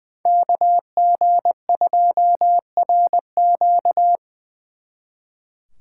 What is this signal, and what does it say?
Morse code, heard as a keyed tone, about 24 wpm, 700 Hz. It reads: KG2RQ
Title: Speed Racing - USA 2x2 Call Signs 20wpm